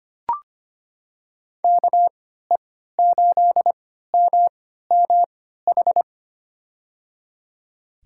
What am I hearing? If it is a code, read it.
Morse code, heard as a keyed tone, about 25 wpm, spaced as for 14 wpm, 700 Hz. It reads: KE8MMH
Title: USA Call Signs - 6 Characters - 3x Character Spacing - 25wpm